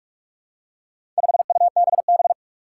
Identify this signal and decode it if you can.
Morse code, heard as a keyed tone, about 44 wpm, 700 Hz. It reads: 5UBB
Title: Four Challenging Letter-Number Combinations 45wpm